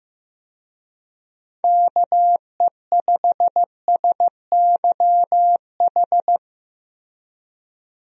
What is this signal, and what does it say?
Morse code, heard as a keyed tone, about 15 wpm, 700 Hz. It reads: KE5SYH
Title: USA Call Signs - 6 Characters - Repeated 2x - 15wpm